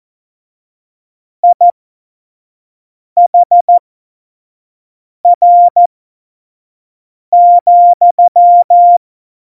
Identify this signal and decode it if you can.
Morse code, heard as a keyed tone, about 14 wpm, spaced as for 5 wpm, 700 Hz. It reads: IHR,
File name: Morsekurs / Witze / Witz 21 Juli 2020, 25 F70